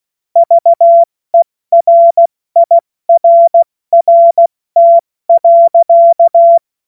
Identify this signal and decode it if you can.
Morse code, heard as a keyed tone, about 16 wpm, 680 Hz. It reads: VERIRRT.
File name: Morsekurs / Witze / Witz 03. Mai 2022 70 F80